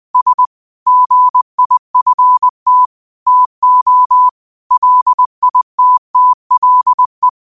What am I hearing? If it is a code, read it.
Morse code, heard as a keyed tone, about 20 wpm, 1000 Hz. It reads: S GIFT TO LITTLE
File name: qotdfiles / 20wpm